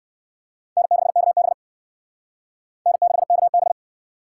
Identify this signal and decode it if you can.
Morse code, heard as a keyed tone, about 56 wpm, 700 Hz. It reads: N6KB N6KB
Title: Speed Racing - USA Call Signs 45wpm